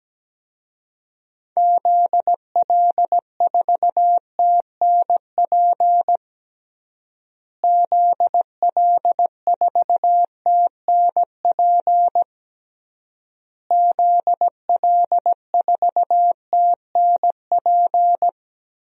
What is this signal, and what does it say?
Morse code, heard as a keyed tone, about 17 wpm, 700 Hz. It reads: ZL4TNP ZL4TNP ZL4TNP
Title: DXCC-100 Call Signs - 6 Characters - Repeated 3x - 17wpm